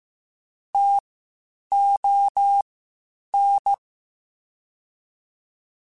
Morse code, heard as a keyed tone, about 15 wpm, spaced as for 8 wpm, 800 Hz. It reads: TON
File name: Spaced Alphas / Spaced Alphas mp3s / Ts sp @13